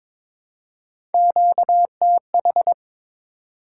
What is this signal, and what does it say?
Morse code, heard as a keyed tone, about 22 wpm, 700 Hz. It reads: QTH
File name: practice / 22wpm / Top 20 QSO Elements - 22wpm